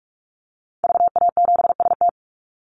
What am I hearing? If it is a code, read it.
Morse code, heard as a keyed tone, about 44 wpm, 700 Hz. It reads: VA7ST
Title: Call Signs Repeated 3x - 45wpm